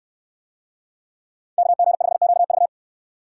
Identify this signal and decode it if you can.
Morse code, heard as a keyed tone, about 68 wpm, 700 Hz. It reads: KG5QV